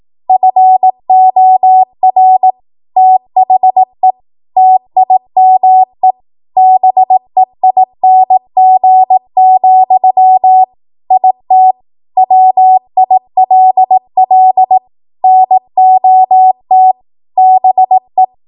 Morse code, heard as a keyed tone, about 18 wpm, 750 Hz. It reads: FOR THE TIME BEING, IT WILL NOT BE
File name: Morse / 18WPM